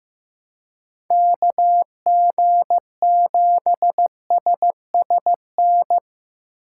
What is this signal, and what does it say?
Morse code, heard as a keyed tone, about 15 wpm, 700 Hz. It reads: KG7SSN
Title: USA Call Signs - 6 Characters - 2x Character Spacing - 15wpm